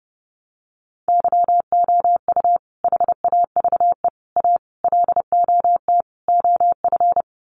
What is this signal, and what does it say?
Morse code, heard as a keyed tone, about 30 wpm, 700 Hz. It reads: YOU HAVE A LOT OF
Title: Sets of 5 Words Encore 30wpm